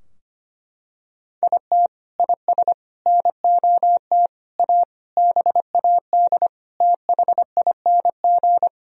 Morse code, heard as a keyed tone, about 25 wpm, 700 Hz. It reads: IT IS NOT A BAD THING
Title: Sentences from Top 1000 Words - Review Part 4 - 25wpm